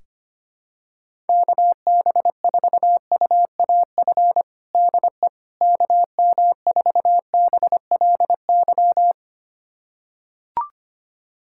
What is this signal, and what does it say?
Morse code, heard as a keyed tone, about 25 wpm, 700 Hz. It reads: KB4UAF DE KM4BLY
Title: USA Callsign DE Callsign 25wpm